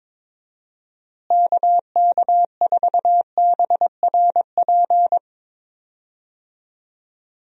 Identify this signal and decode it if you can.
Morse code, heard as a keyed tone, about 22 wpm, 700 Hz. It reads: KK4BRP
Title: USA Call Signs - 6 Characters - Repeated 2x - 22wpm